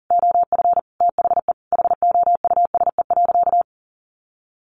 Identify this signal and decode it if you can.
Morse code, heard as a keyed tone, about 40 wpm, 700 Hz. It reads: OF THE HOUSE.
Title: Sentences from Top 400 Words 40wpm